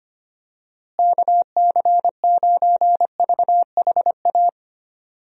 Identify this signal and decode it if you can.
Morse code, heard as a keyed tone, about 25 wpm, 700 Hz. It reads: KC9VHA
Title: USA Call Signs - 6 Characters - Repeated 3x - 25wpm